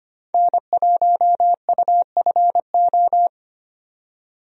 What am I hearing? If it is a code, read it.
Morse code, heard as a keyed tone, about 25 wpm, 700 Hz. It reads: N1UFO
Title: USA Call Signs - 5 Characters - 25wpm